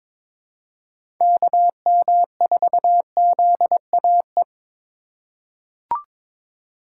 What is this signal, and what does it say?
Morse code, heard as a keyed tone, about 22 wpm, 700 Hz. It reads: KM4ZAE
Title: USA Call Signs - 6 Characters - 3x Character Spacing - 22wpm